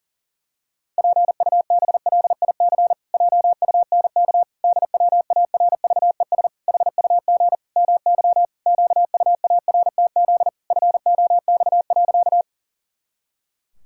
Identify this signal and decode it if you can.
Morse code, heard as a keyed tone, about 40 wpm, 700 Hz. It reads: PUBLIC JUNK DWARVES HUG MY QUARTZ FOX.
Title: Pangram Challenge v2 - 40wpm